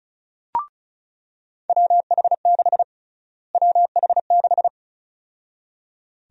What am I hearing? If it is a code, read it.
Morse code, heard as a keyed tone, about 34 wpm, 700 Hz. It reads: WH6 WH6
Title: USA Call Sign Prefixes - Repeated 2x - 35wpm